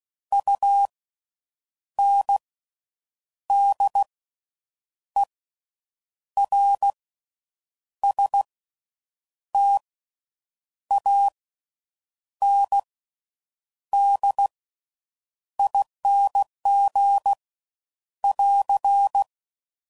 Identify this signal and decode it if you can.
Morse code, heard as a keyed tone, about 16 wpm, spaced as for 7 wpm, 800 Hz. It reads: UNDERSTANDING <AR>
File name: Slow With Faster Ings